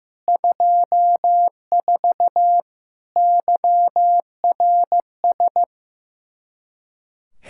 Morse code, heard as a keyed tone, about 15 wpm, 700 Hz. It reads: 24 YRS
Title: Practice QSO - Exchange 3 - More Info - 15wpm